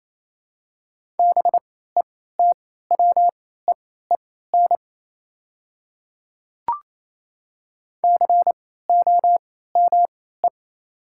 Morse code, heard as a keyed tone, about 27 wpm, spaced as for 16 wpm, 700 Hz. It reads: BETWEEN COME
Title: 3x Character Spacing - Top 500 Words - 28wpm